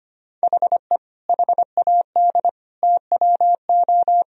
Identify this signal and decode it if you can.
Morse code, heard as a keyed tone, about 25 wpm, 700 Hz. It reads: HE HAD TWO
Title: Sentences from Top 1000 Words - Review Part 1 - 25wpm